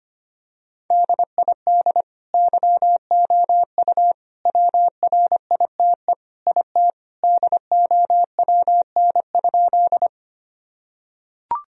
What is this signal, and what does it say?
Morse code, heard as a keyed tone, about 25 wpm, 700 Hz. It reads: DID YOU WRITE IT DOWN?
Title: Sentences from Top 1000 Words - Review Part 1 - 25wpm